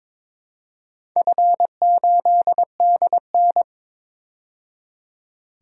Morse code, heard as a keyed tone, about 22 wpm, 700 Hz. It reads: F8DN
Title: DXCC-100 Call Signs - 4 Characters - 22wpm